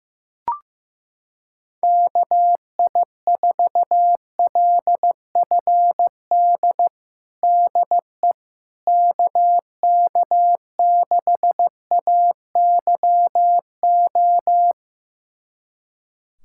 Morse code, heard as a keyed tone, about 16 wpm, 700 Hz. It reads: KI4LFD DE KK6AYO
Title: USA Callsign DE Callsign 15wpm